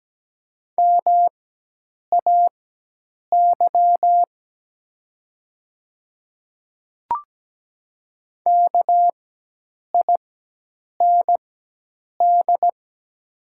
Morse code, heard as a keyed tone, about 17 wpm, spaced as for 8 wpm, 700 Hz. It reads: MAY KIND
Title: 4x Character Spacing - Top 200 Words - 17wpm